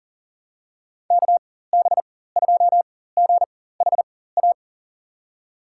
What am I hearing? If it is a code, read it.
Morse code, heard as a keyed tone, about 39 wpm, spaced as for 19 wpm, 700 Hz. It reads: KB2GHA
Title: USA Call Signs - 6 Characters - 4x Character Spacing - 40wpm